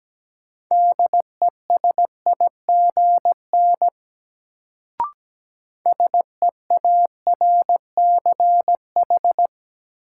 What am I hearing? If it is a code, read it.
Morse code, heard as a keyed tone, about 17 wpm, 700 Hz. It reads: DESIGN SEARCH
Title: Sentences from Top 800 Words - New Word Drill - 17wpm